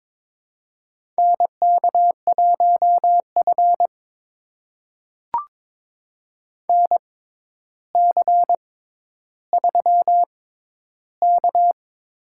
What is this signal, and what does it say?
Morse code, heard as a keyed tone, about 22 wpm, spaced as for 11 wpm, 700 Hz. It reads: NK1F NC3K
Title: USA Call Signs - 4 Characters - 6x Character Spacing - 22wpm